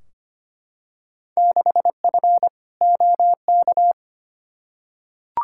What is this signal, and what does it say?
Morse code, heard as a keyed tone, about 25 wpm, 700 Hz. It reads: 6F OK
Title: Field Day Exchanges Encore - 25wpm